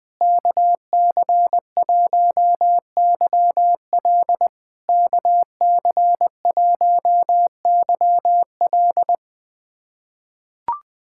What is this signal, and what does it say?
Morse code, heard as a keyed tone, about 20 wpm, 690 Hz. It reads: KC1YL KC1YL
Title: POTA QSO Practice - 20wpm